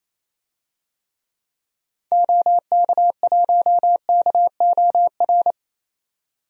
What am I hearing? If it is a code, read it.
Morse code, heard as a keyed tone, about 28 wpm, 700 Hz. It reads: OK1KOR
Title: DXCC-100 Call Signs - 6 Characters - 28wpm